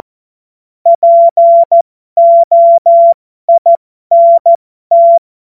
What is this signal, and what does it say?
Morse code, heard as a keyed tone, about 14 wpm, spaced as for 12 wpm, 680 Hz. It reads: POINT